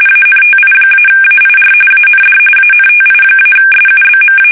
RTTY, about 45.45 baud. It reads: ANTI-AIR AND ANTI-SUBMA
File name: fsk850 / battleshipnj